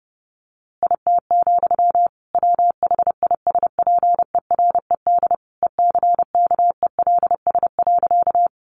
Morse code, heard as a keyed tone, about 30 wpm, 700 Hz. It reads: IT, WHISPERED ECKELS.
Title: A Sound Of Thunder 30wpm